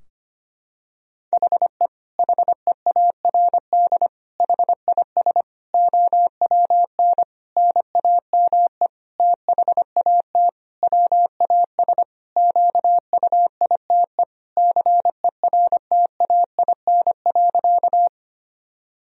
Morse code, heard as a keyed tone, about 25 wpm, 700 Hz. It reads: HE HEARD HIS OWN NAME THAT WAS QUITE CERTAIN.
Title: Sentences from Top 1000 Words - Review Part 2 - 25wpm